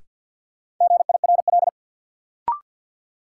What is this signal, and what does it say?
Morse code, heard as a keyed tone, about 46 wpm, 700 Hz. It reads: GIRL
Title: Sentences from Top 400 Words - New Word Drill - 50wpm